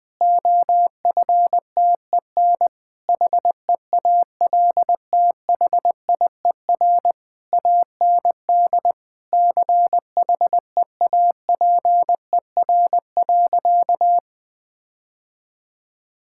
Morse code, heard as a keyed tone, about 20 wpm, 700 Hz. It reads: OFTEN HEALTHIER AND CHEAPER.
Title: Difficult Sentences with Hints - 20wpm